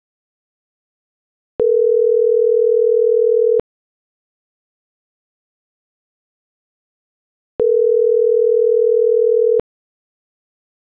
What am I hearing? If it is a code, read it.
Ringback tone (North America), 440 + 480 Hz.